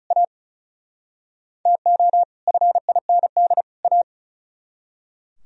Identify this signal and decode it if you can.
Morse code, heard as a keyed tone, about 45 wpm, 700 Hz. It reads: A TO FIND A